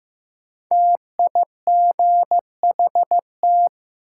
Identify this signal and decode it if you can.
Morse code, heard as a keyed tone, about 15 wpm, 700 Hz. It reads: TIGHT
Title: Five Letter Words - 15wpm